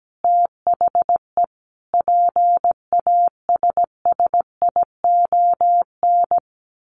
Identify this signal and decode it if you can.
Morse code, heard as a keyed tone, about 17 wpm, 700 Hz. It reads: THE PASSION
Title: Sentences from Top 1400 Words 17wpm